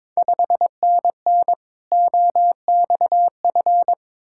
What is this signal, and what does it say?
Morse code, heard as a keyed tone, about 22 wpm, 700 Hz. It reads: 5NN OXF
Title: Ontario QSO Party Exchange 22wpm